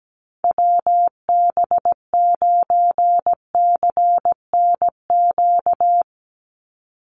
Morse code, heard as a keyed tone, about 17 wpm, 700 Hz. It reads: WB9CNQ
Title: USA Call Signs 17wpm